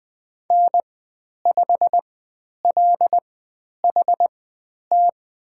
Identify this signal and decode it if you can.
Morse code, heard as a keyed tone, about 20 wpm, spaced as for 10 wpm, 700 Hz. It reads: N5LHT
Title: 095 Lesson 16g - Letter c - Call Signs 20wpm